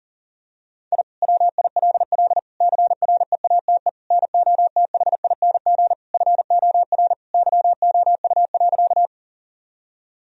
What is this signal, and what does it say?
Morse code, heard as a keyed tone, about 37 wpm, 700 Hz. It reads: I WILL CREATE NOTHING FOR YOU.
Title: Sentences from Top 1000 Words - Review Part 3 - 40wpm